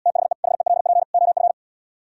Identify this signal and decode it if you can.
Morse code, heard as a keyed tone, about 79 wpm, 700 Hz. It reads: THE HELP OF